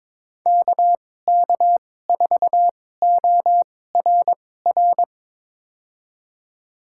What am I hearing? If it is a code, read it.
Morse code, heard as a keyed tone, about 21 wpm, spaced as for 16 wpm, 700 Hz. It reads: KK4ORR
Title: USA Call Signs - 6 Characters - 2x Character Spacing - 22wpm